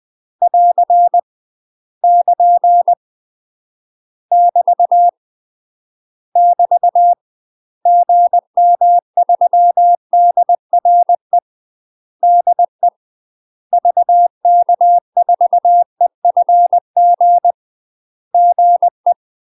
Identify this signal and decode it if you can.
Morse code, heard as a keyed tone, about 20 wpm, 700 Hz. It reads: <AR> <KN> <BT> <BT> GM3DRE DE VK4EFG GE
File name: UK MASTER QSOs TRAINING / 20WPM / UK MASTER QSOs 0001